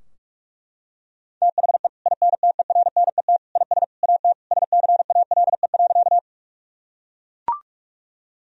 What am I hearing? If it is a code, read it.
Morse code, heard as a keyed tone, about 41 wpm, 700 Hz. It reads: THE INTERNET IS AT SCALE.
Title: Sentences from Top 1000 Words - Review Part 4 - 45wpm